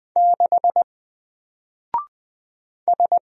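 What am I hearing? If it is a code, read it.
Morse code, heard as a keyed tone, about 20 wpm, 700 Hz. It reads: 6 S
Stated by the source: Single letter and number Morse drill at 20 wpm, from N4LCD.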